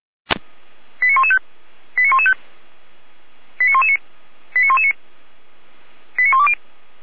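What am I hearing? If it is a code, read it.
Five-tone selective call (ZVEI), heard as quick runs of five tones, tones of about 70 ms. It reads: 89115 89115 89119 89119 89120
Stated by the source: ZVEI five-tone selective call 89067 received on SDR, from the audio archive of feuerwehr-orlamuende (fire brigade Orlamünde).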